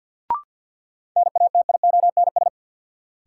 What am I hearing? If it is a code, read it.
Morse code, heard as a keyed tone, about 52 wpm, 700 Hz. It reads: NATIONS
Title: Sets of 3 Words - New Word Drill - 50wpm